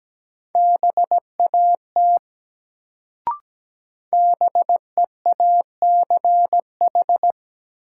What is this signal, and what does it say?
Morse code, heard as a keyed tone, about 17 wpm, 700 Hz. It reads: BAT BEACH
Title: Sentences from Top 1500 Words - New Word Drill - 17wpm